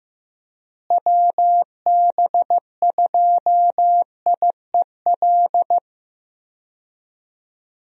Morse code, heard as a keyed tone, about 15 wpm, 700 Hz. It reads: WB2IEL
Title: USA Call Signs - 6 Characters - Repeated 2x - 15wpm